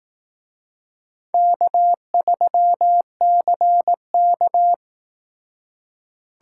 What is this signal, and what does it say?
Morse code, heard as a keyed tone, about 18 wpm, 700 Hz. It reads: K3CK
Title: Speed Racing - USA 1x2 Call Signs 15wpm